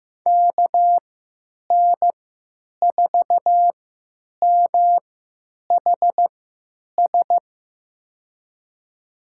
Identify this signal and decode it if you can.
Morse code, heard as a keyed tone, about 15 wpm, spaced as for 9 wpm, 700 Hz. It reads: KN4MHS
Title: USA Call Signs - 6 Characters - 3x Character Spacing - 15wpm